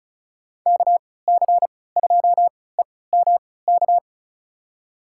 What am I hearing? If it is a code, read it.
Morse code, heard as a keyed tone, about 36 wpm, spaced as for 20 wpm, 700 Hz. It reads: KC2EMK
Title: USA Call Signs - 3x Character Spacing - 35wpm